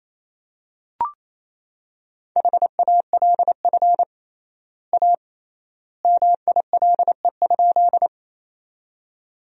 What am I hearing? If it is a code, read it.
Morse code, heard as a keyed tone, about 28 wpm, 700 Hz. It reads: HALF A MILE?
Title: 3x Word-Spacing - Sentences from Top 400 Words - 28wpm